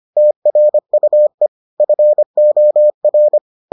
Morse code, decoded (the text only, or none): TRUE FOR